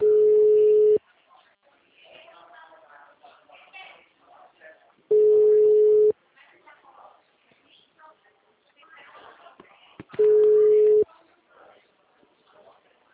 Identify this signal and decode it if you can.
Ringback tone (Europe, 425 Hz), with speech.